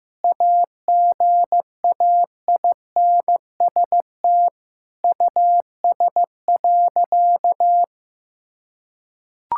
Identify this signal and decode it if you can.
Morse code, heard as a keyed tone, about 16 wpm, 700 Hz. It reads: AGAINST US.
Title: Sentences from Top 1000 Words - Review Part 6 - 15wpm